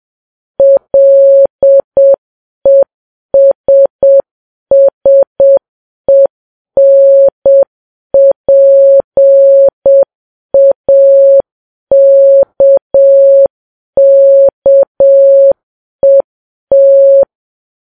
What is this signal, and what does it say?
Morse code, heard as a keyed tone, about 7 wpm, 550 Hz. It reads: LESSENPAKKET
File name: Files Web tls / Morse on9fv / morse7kps